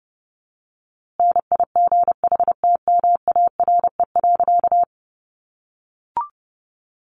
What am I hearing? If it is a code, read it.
Morse code, heard as a keyed tone, about 30 wpm, 700 Hz. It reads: NIGHTMARE.
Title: A Sound Of Thunder 30wpm